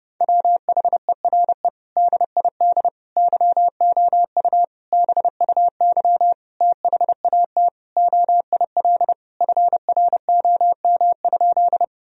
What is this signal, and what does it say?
Morse code, heard as a keyed tone, about 32 wpm, 700 Hz. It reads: WHERE DID YOU BUY THAT OIL FROM?
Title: Sentences from Top 1000 Words - Review Part 4 - 30wpm